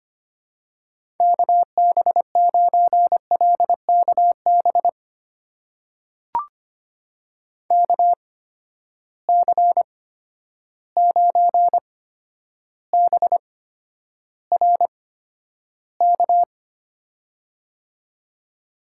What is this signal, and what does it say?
Morse code, heard as a keyed tone, about 25 wpm, spaced as for 11 wpm, 700 Hz. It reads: KB9LKB KC9BRK
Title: USA Call Signs - 6 Characters - 8x Character Spacing - 25wpm